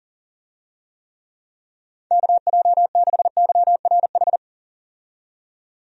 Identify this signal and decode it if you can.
Morse code, heard as a keyed tone, about 40 wpm, 700 Hz. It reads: KJ6YRH
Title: USA Call Signs - 6 Characters - 40wpm